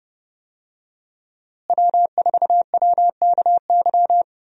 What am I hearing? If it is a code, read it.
Morse code, heard as a keyed tone, about 30 wpm, 700 Hz. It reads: W4WKY
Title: USA Call Signs - 5 Characters - Repeated 2x - 30wpm